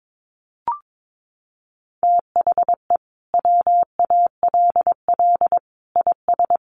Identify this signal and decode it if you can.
Morse code, heard as a keyed tone, about 22 wpm, 700 Hz. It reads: THE WALL IS
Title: Sentences from Top 1400 Words 22wpm